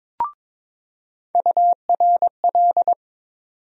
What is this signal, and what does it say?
Morse code, heard as a keyed tone, about 22 wpm, 700 Hz. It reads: URL
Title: USA Call Sign Postfixes - 3 Letters 22wpm